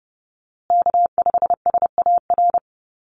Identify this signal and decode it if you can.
Morse code, heard as a keyed tone, about 30 wpm, 700 Hz. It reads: K5SAR
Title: USA Call Signs Repeated 3x 30wpm